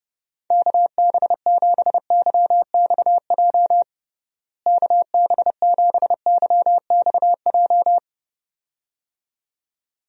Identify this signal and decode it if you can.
Morse code, heard as a keyed tone, about 30 wpm, 700 Hz. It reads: KB7YXJ KB7YXJ
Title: USA Call Signs - 6 Characters - Repeated 2x - 30wpm